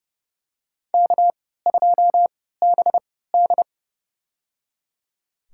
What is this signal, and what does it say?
Morse code, heard as a keyed tone, about 32 wpm, spaced as for 17 wpm, 700 Hz. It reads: K2BD
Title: USA Call Signs - 4 Characters - 3x Character Spacing - 30wpm